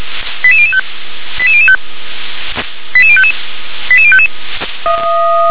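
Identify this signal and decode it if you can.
Five-tone selective call (ZVEI), heard as quick runs of five tones, tones of about 80 ms. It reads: 88005 88005 88055 88055
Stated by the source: Five-tone ZVEI selective call received on an SDR.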